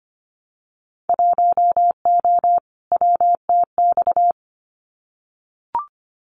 Morse code, heard as a keyed tone, about 25 wpm, 700 Hz. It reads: 1O WTX
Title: Winter Field Day Exchanges Encore 25wpm